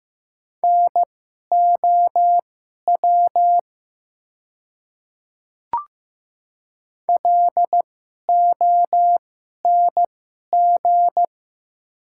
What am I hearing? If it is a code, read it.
Morse code, heard as a keyed tone, about 15 wpm, spaced as for 11 wpm, 700 Hz. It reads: NOW LONG
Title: 2x Character Spacing - Top 200 Words - 15wpm